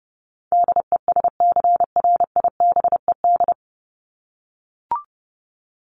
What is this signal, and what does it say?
Morse code, heard as a keyed tone, about 30 wpm, 700 Hz. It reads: DESCRIBED
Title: ED Suffix - Top 100 Words with Base Words 30wpm